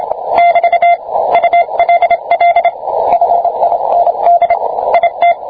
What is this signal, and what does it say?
Morse code, heard as a keyed tone, about 27 wpm, 690 Hz, with radio noise. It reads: <BT> ULL FIND IT